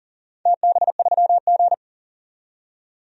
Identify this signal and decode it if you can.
Morse code, heard as a keyed tone, about 40 wpm, 700 Hz. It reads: TB3G
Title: DXCC-100 Call Signs - 4 Characters - 40wpm